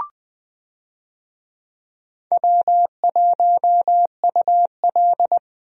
Morse code, heard as a keyed tone, about 20 wpm, 700 Hz. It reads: W1UL
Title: USA Call Signs - 4 Characters - Repeated 2x - 20wpm